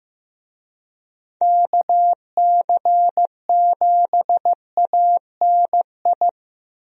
Morse code, heard as a keyed tone, about 15 wpm, 700 Hz. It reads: KC7ANI
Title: USA Call Signs - 6 Characters - Repeated 3x - 15wpm